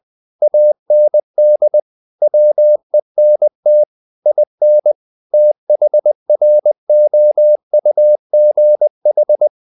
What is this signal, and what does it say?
Morse code, heard as a keyed tone, about 20 wpm, 600 Hz. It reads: AND WENT IN THROUGH
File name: the old man and the sea / the old man and the sea part 1 20wpm